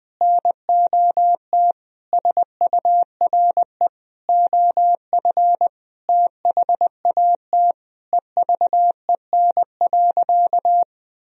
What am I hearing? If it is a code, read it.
Morse code, heard as a keyed tone, about 21 wpm, 700 Hz. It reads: NOT SURE OF THAT EVEN.
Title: Sentences from Top 1000 Words - Review Part 1 - 20wpm